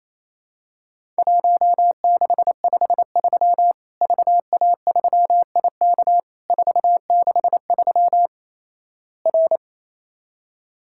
Morse code, heard as a keyed tone, about 28 wpm, 700 Hz. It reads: 1653 VA3IK 463 R
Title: Worked All Europe DX Contest - QTC Exchange - 28wpm